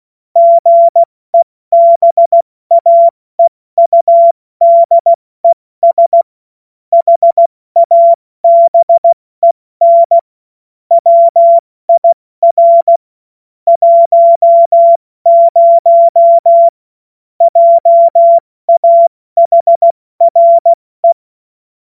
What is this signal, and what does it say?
Morse code, heard as a keyed tone, about 16 wpm, 680 Hz. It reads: GEBAEUDES HABEN WIR 10 JAHRE
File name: Morsekurs / Witze / Witz 30. Dezember 2025 70 F80